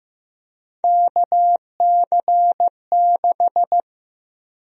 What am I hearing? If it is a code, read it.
Morse code, heard as a keyed tone, about 15 wpm, 700 Hz. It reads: KC6
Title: USA Call Sign Prefixes - 15wpm